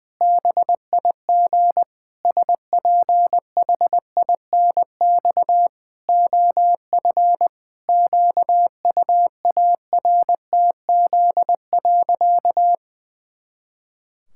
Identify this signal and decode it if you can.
Morse code, heard as a keyed tone, about 20 wpm, 700 Hz. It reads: BIG SPHINX OF QUARTZ.
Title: Pangram Challenge v2 - 20wpm